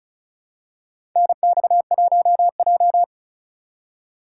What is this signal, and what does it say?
Morse code, heard as a keyed tone, about 36 wpm, 700 Hz. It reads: NX1J
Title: USA Call Signs - 2x Character Spacing - 35wpm